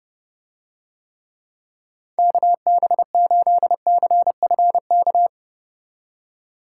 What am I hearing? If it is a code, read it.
Morse code, heard as a keyed tone, about 30 wpm, 700 Hz. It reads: KB8CFK